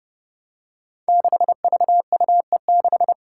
Morse code, heard as a keyed tone, about 28 wpm, 700 Hz. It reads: 6VUE6
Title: Five Challenging Letter-Number Combinations 30wpm